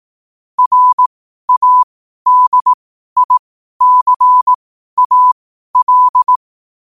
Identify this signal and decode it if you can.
Morse code, heard as a keyed tone, about 18 wpm, spaced as for 12 wpm, 1000 Hz. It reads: RADICAL
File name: qotdfiles / 12wpm